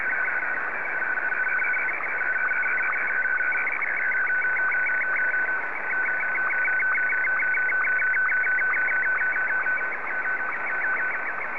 SSTV (slow-scan television), Martin 1.